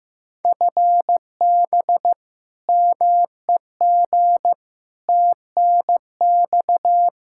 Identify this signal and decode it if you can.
Morse code, heard as a keyed tone, about 15 wpm, 700 Hz. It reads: FB MEG TNX
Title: Practice QSO - Exchange 3 - More Info - 15wpm